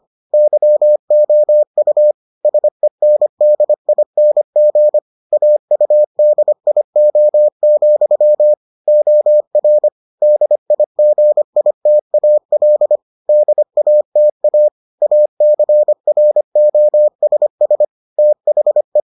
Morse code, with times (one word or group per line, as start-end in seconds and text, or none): 0.3-2.1 YOU
2.5-5.0 SENDING
5.3-8.5 AUDIO,
8.9-9.9 OR
10.2-13.0 DIGITAL
13.3-14.7 DATA
15.0-17.9 ACROSS
18.2-19.0 THE